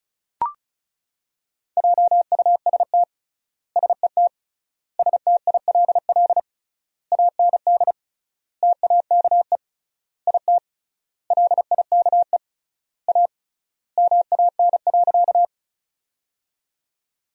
Morse code, heard as a keyed tone, about 36 wpm, 700 Hz. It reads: JUST SET STILL AND TAKE IT LIKE A MAN.
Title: 3x Word-Spacing - Sentences from Top 300 Words - 35wpm